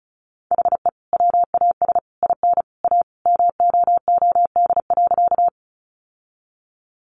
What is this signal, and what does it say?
Morse code, heard as a keyed tone, about 34 wpm, 700 Hz. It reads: HE WAS IN A MOOD.
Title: Sentences from Top 1100 Words 35wpm